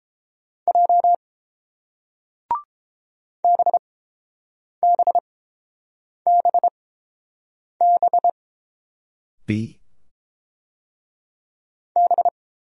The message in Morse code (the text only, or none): J BBBB B